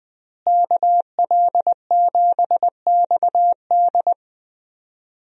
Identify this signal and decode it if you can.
Morse code, heard as a keyed tone, about 20 wpm, 700 Hz. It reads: KL7XD
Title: DXCC-100 Call Signs - 5 Characters - Repeated 3x - 20wpm